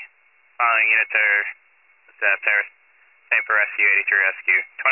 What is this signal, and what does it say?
Emergency Alert System (SAME) tones, heard as harsh buzzing bursts, with speech. It is heard from 1.1 to 3.9 s.